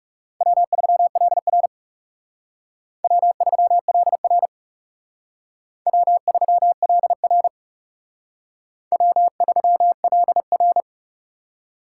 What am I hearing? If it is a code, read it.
Morse code, heard as a keyed tone, about 36 wpm, 700 Hz. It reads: W3LR W3LR W3LR W3LR
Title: Speed Racing - USA 1x2 Call Signs 30wpm